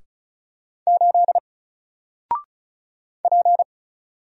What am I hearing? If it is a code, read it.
Morse code, heard as a keyed tone, about 35 wpm, 700 Hz. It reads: 8 P
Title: Single Letter-Number Alt 35wpm